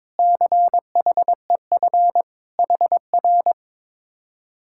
Morse code, heard as a keyed tone, about 22 wpm, 700 Hz. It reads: CHEF HR
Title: Speed-Racing Common QSO Phrases - 22wpm